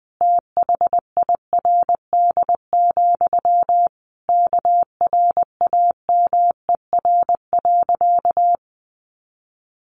Morse code, heard as a keyed tone, about 20 wpm, 700 Hz. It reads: THIRD, KRAMER.
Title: A Sound Of Thunder 20wpm